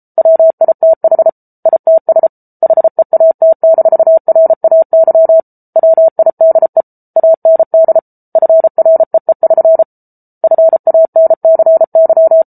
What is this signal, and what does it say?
Morse code, heard as a keyed tone, about 34 wpm, 650 Hz. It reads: WITH ITS HEAT-RAY WIDE AND FREE<SN> FANCY